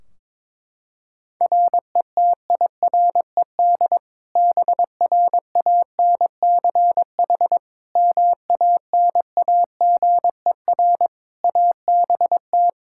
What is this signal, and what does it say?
Morse code, heard as a keyed tone, about 22 wpm, 700 Hz. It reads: RETIRED BRANCH MANAGER ABT
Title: Practice QSO - Exchange 3 - More Info - 22wpm